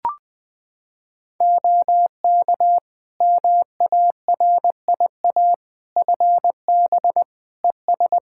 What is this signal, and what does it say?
Morse code, heard as a keyed tone, about 21 wpm, 700 Hz. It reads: OK MARIA FB ES
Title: Practice QSO - Exchange 2 - Info - 20wpm